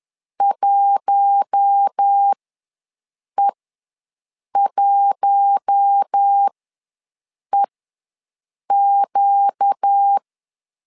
Morse code, heard as a keyed tone, about 10 wpm, spaced as for 6 wpm, 800 Hz. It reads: 1E1EQ
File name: LEZIONI / 4 Q1 ws 10wpm 2538